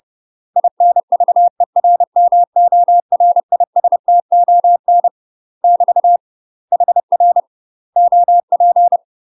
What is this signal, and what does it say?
Morse code, heard as a keyed tone, about 30 wpm, 700 Hz. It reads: INVERMORISTON <BT> HR OP
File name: UK MASTER QSOs TRAINING / 30WPM / UK MASTER QSOs 0030